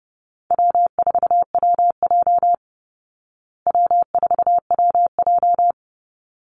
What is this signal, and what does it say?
Morse code, heard as a keyed tone, about 30 wpm, 700 Hz. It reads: W4WJ W4WJ
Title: Easy Call Signs Repeated 2x - 30wpm